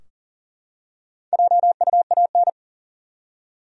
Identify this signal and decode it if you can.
Morse code, heard as a keyed tone, about 40 wpm, 700 Hz. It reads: JUAN
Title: USA Common Names - Max 4 Letters - 40wpm